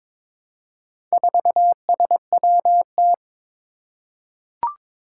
Morse code, heard as a keyed tone, about 23 wpm, 700 Hz. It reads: 4SWT
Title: Four Challenging Letter-Number Combinations 22wpm